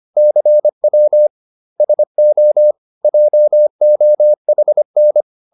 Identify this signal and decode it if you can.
Morse code, heard as a keyed tone, about 25 wpm, 600 Hz. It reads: CW SO JOHN